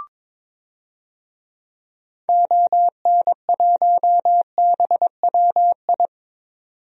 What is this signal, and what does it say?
Morse code, heard as a keyed tone, about 22 wpm, 700 Hz. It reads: ON1BWI
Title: DXCC-100 Call Signs - 6 Characters - 22wpm